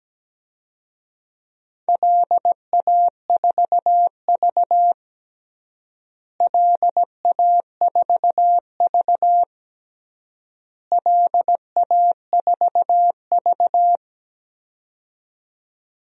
Morse code, heard as a keyed tone, about 17 wpm, 700 Hz. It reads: LA4V LA4V LA4V
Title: DXCC-100 Call Signs - 4 Characters - Repeated 3x - 17wpm